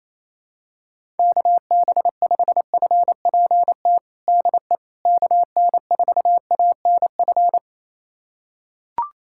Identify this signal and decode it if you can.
Morse code, heard as a keyed tone, about 27 wpm, 700 Hz. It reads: KB5FPT DE KN4ANF
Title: USA Callsign DE Callsign - 6 Characters - 28wpm